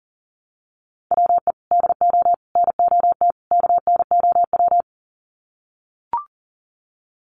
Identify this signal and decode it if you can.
Morse code, heard as a keyed tone, about 40 wpm, 700 Hz. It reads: WE DO NOT KNOW
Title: Sets of 4 Words 40wpm